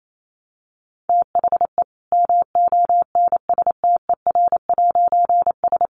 Morse code, heard as a keyed tone, about 28 wpm, 700 Hz. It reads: THE MONSTER'S
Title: A Sound Of Thunder 28wpm